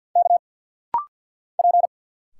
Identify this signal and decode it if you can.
Morse code, heard as a keyed tone, about 50 wpm, 700 Hz. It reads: K P
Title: Single Letters - Rapid-Fire 50wpm